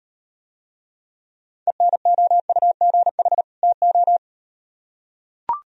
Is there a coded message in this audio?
Yes: Morse code from 1.7 to 4.2 s, heard as a keyed tone, about 37 wpm, 700 Hz. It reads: ENOUGH TO